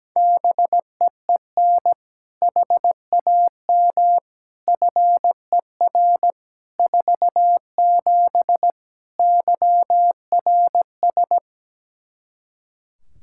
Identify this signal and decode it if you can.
Morse code, heard as a keyed tone, about 17 wpm, 700 Hz. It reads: BEEN HAM FER 47 YRS
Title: Practice QSO - Exchange 3 - More Info - 17wpm